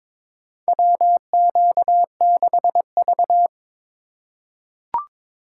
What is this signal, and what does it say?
Morse code, heard as a keyed tone, about 21 wpm, 700 Hz. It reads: WQ6V
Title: USA Call Signs - 4 Characters - 6x Character Spacing - 22wpm